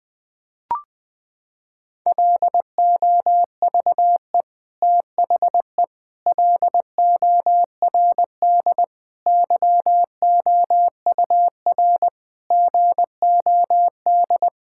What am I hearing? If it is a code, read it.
Morse code, heard as a keyed tone, about 21 wpm, 700 Hz. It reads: LOVE THE LORD YOUR GOD
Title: Sentences with 9-Word Phrases - 20wpm